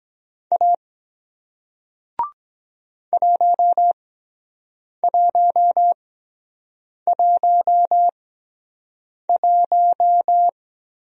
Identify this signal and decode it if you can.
Morse code, heard as a keyed tone, about 23 wpm, spaced as for 7 wpm, 700 Hz. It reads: A 1111